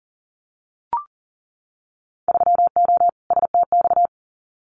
Morse code, heard as a keyed tone, about 40 wpm, 700 Hz. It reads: 3O STX
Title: Winter Field Day Exchanges 40wpm